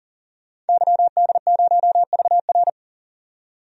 Morse code, heard as a keyed tone, about 40 wpm, 700 Hz. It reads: YD0VR
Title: DXCC-100 Call Signs - 5 Characters - 40wpm